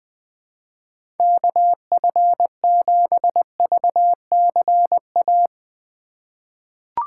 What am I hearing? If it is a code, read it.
Morse code, heard as a keyed tone, about 20 wpm, 700 Hz. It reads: KF7VCA
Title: USA Call Signs - 6 Characters - 8x Character Spacing - 20wpm